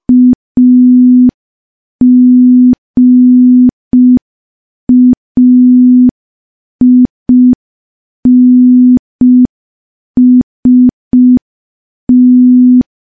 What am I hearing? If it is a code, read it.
Morse code, heard as a keyed tone, about 5 wpm, 250 Hz. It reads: AGAINST
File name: morse / morse-training-05WPM